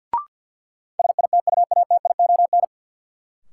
Morse code, heard as a keyed tone, about 51 wpm, 700 Hz. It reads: SITUATION
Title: Sentences from Top 1200 Words - New Word Drill Alt - 50wpm